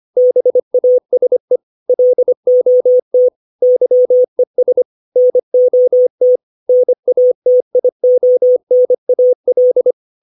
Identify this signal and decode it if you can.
Morse code, heard as a keyed tone, about 25 wpm, 500 Hz. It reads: BASE LOT YES NOT NATIONAL
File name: 632 words / 632 Words 25WPM